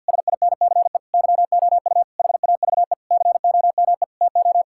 Morse code, heard as a keyed tone, about 50 wpm, 700 Hz. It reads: SINCE YOU HAVE COME TO
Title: Sentences from Top 1600 Words 50wpm